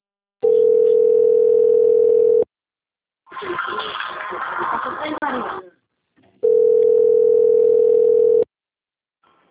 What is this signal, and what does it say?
Ringback tone (North America), 440 + 480 Hz, with speech.
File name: out- - - - - .17110